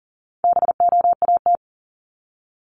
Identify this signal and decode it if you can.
Morse code, heard as a keyed tone, about 40 wpm, 700 Hz. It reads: BOAT
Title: Top 500 Words 40wpm